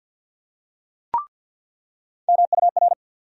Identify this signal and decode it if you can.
Morse code, heard as a keyed tone, about 48 wpm, 700 Hz. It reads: MUR
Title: USA Call Sign Postfixes - 3 Letters 50wpm